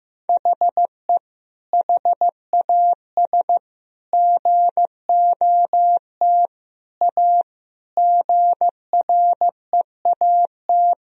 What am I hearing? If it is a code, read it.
Morse code, heard as a keyed tone, about 15 wpm, 700 Hz. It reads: HE HAS GOT A GREAT
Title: Sentences from Top 1000 Words - Review Part 1 - 15wpm